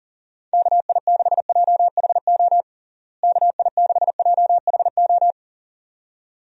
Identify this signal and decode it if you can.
Morse code, heard as a keyed tone, about 40 wpm, 700 Hz. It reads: KI6JHO KI6JHO
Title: USA Call Signs - 6 Characters - Repeated 2x - 40wpm